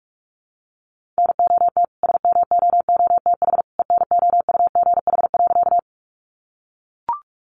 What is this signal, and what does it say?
Morse code, heard as a keyed tone, about 44 wpm, 700 Hz. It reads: NOT SMOOTH ENOUGH.
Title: Sentences from Top 1400 Words 45wpm